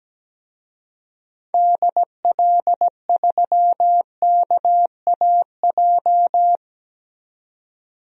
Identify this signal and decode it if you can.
Morse code, heard as a keyed tone, about 17 wpm, 700 Hz. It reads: DL3KAJ
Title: DXCC-100 Call Signs - 6 Characters - 17wpm